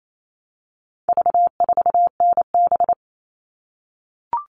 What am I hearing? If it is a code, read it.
Morse code, heard as a keyed tone, about 28 wpm, 700 Hz. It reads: V4NB
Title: All Cut Numbers - 4-Digit 28wpm